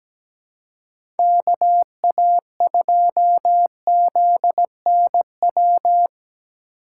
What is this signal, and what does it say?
Morse code, heard as a keyed tone, about 17 wpm, 700 Hz. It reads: KA2ZNW
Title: USA Call Signs - 6 Characters - 3x Character Spacing - 17wpm